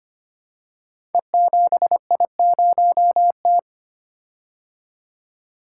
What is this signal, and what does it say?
Morse code, heard as a keyed tone, about 25 wpm, 700 Hz. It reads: E7I0T
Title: Code Groups - 5 Letter:Numbers - 25wpm